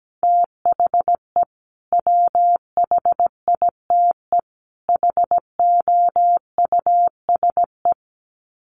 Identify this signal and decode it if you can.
Morse code, heard as a keyed tone, about 17 wpm, 700 Hz. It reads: THE WHITE HOUSE
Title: Sets of 3 Words 17wpm